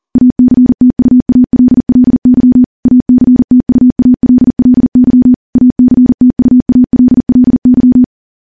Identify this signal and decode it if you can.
Morse code, heard as a keyed tone, about 42 wpm, 250 Hz. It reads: ACTUALLY ACTUALLY ACTUALLY
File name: morse / morse-training-40WPM